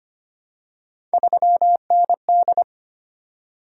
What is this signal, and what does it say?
Morse code, heard as a keyed tone, about 25 wpm, 700 Hz. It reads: 3ND
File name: Three Letter-Number Combinations - 25wpm